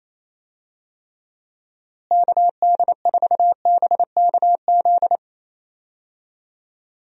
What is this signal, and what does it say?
Morse code, heard as a keyed tone, about 28 wpm, 700 Hz. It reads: KD4BKZ
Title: USA Call Signs - 6 Characters - Repeated 3x - 28wpm